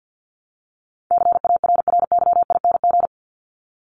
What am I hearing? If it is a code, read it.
Morse code, heard as a keyed tone, about 50 wpm, 700 Hz. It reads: CARRYING
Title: ING Suffix - Top 100 Words with Base Words 50wpm